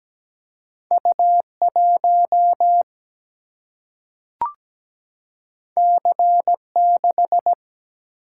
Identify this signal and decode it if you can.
Morse code, heard as a keyed tone, about 17 wpm, 700 Hz. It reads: U 1 C 6
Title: Single-Letter Single-Number 17wpm